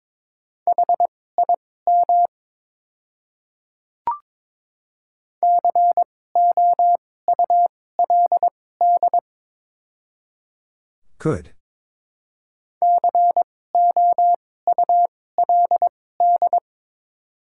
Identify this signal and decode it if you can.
Morse code, heard as a keyed tone, about 21 wpm, spaced as for 16 wpm, 700 Hz. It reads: HIM COULD COULD
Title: 2x Character Spacing - Top 100 Words - 22wpm